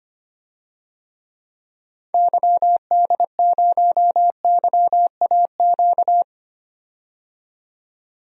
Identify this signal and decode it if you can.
Morse code, heard as a keyed tone, about 25 wpm, 700 Hz. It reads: YD0YAQ
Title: DXCC-100 Call Signs - 6 Characters - 25wpm